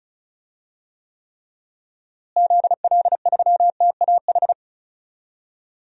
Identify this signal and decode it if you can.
Morse code, heard as a keyed tone, about 36 wpm, 700 Hz. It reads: ZL3TAH
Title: DXCC-100 Call Signs - 6 Characters - Repeated 3x - 35wpm